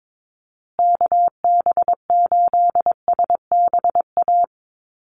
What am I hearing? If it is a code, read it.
Morse code, heard as a keyed tone, about 22 wpm, 700 Hz. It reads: KB8SBA
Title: USA Call Signs Repeated 3x 22wpm